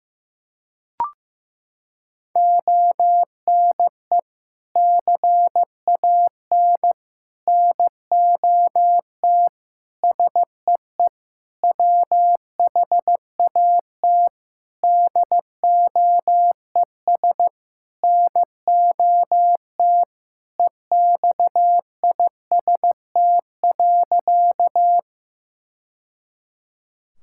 Morse code, heard as a keyed tone, about 15 wpm, 700 Hz. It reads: ONE CAN NOT SEE WHAT DOES NOT EXIST.
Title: Sentences from Top 1000 Words - Review Part 5 - 15wpm